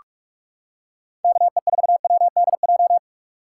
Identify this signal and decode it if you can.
Morse code, heard as a keyed tone, about 37 wpm, 700 Hz. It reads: KE4WDJ
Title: Speed Racing - USA Call Signs 30wpm